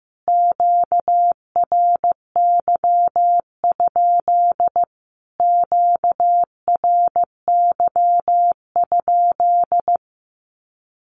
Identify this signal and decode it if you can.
Morse code, heard as a keyed tone, about 15 wpm, 700 Hz. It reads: QRY? QRY?
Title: Q-Codes Repeated 3x 15wpm